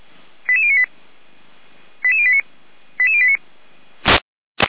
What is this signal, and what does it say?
Five-tone selective call (ZVEI), heard as quick runs of five tones, tones of about 70 ms. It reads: 88098 88089 88089